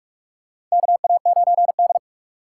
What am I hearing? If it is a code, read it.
Morse code, heard as a keyed tone, about 44 wpm, 700 Hz. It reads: KA9D
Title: USA Call Signs - 4 Characters - 3x Character Spacing - 45wpm